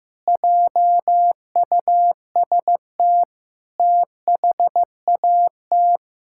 Morse code, heard as a keyed tone, about 16 wpm, 700 Hz. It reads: JUST THAT
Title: Sentences from Top 1600 Words 15wpm